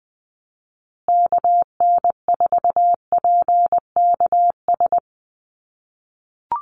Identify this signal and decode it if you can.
Morse code, heard as a keyed tone, about 20 wpm, 700 Hz. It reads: KN4PKS
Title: USA Call Signs Repeated 2x 20wpm